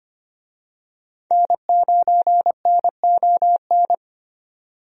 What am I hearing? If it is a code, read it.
Morse code, heard as a keyed tone, about 25 wpm, 700 Hz. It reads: N9NON
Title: USA Call Signs - 5 Characters - 25wpm